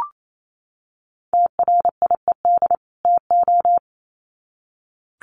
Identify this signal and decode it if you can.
Morse code, heard as a keyed tone, about 28 wpm, 700 Hz. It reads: TRIED TO
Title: Sets of 2 Words 28wpm